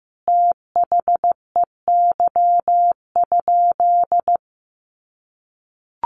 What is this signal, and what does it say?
Morse code, heard as a keyed tone, about 15 wpm, 700 Hz. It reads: THEY?
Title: Sentences from Top 300 Words 15wpm